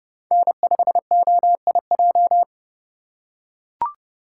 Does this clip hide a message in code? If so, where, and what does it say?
Yes: Morse code from 0.3 to 2.4 s, heard as a keyed tone, about 27 wpm, 700 Hz. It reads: N5OIJ